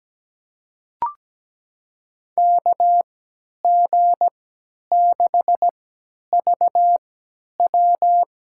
Morse code, heard as a keyed tone, about 17 wpm, spaced as for 10 wpm, 700 Hz. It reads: KG6VW
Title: USA Call Signs - 6 Characters - 3x Character Spacing - 17wpm